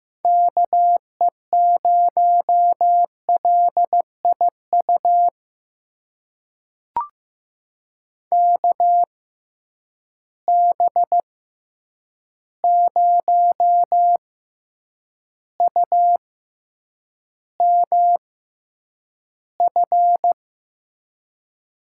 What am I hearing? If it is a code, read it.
Morse code, heard as a keyed tone, about 15 wpm, spaced as for 8 wpm, 700 Hz. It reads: KE0LIU KB0UMF
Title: USA Call Signs - 6 Characters - 6x Character Spacing - 15wpm